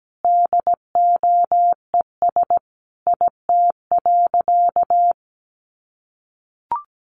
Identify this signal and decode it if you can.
Morse code, heard as a keyed tone, about 17 wpm, 700 Hz. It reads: DOES IT.
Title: Sentences from Top 1500 Words 17wpm